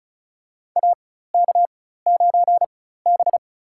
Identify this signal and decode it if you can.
Morse code, heard as a keyed tone, about 34 wpm, spaced as for 16 wpm, 700 Hz. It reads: AK9B
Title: USA Call Signs - 4 Characters - 4x Character Spacing - 35wpm